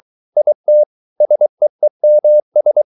Morse code, heard as a keyed tone, about 23 wpm, 600 Hz. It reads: IT SEEMS